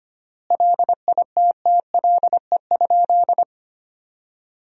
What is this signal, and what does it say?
Morse code, heard as a keyed tone, about 25 wpm, 700 Hz. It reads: LITTLE?
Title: 3x Word-Spacing - Sentences from Top 500 Words - 25wpm